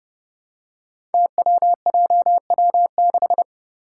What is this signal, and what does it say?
Morse code, heard as a keyed tone, about 29 wpm, 700 Hz. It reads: TWJW6
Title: Five Challenging Letter-Number Combinations 30wpm